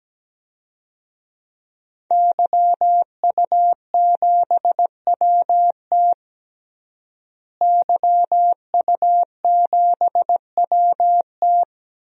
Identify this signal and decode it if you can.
Morse code, heard as a keyed tone, about 17 wpm, 700 Hz. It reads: YU7WT YU7WT
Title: DXCC-100 Call Signs - 5 Characters - Repeated 2x - 17wpm